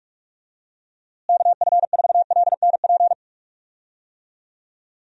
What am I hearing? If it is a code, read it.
Morse code, heard as a keyed tone, about 41 wpm, 700 Hz. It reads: KF4LNP
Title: USA Call Signs - 6 Characters - Repeated 3x - 45wpm